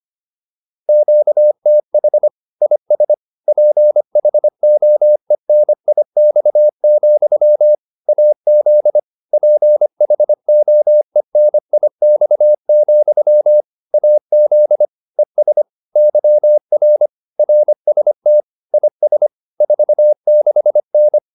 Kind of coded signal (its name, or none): Morse code